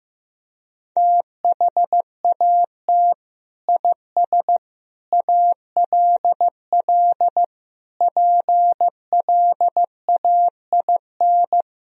Morse code, heard as a keyed tone, about 15 wpm, 700 Hz. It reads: THAT IS ALL PLAIN
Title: Sentences from Top 1000 Words - Review Part 2 - 15wpm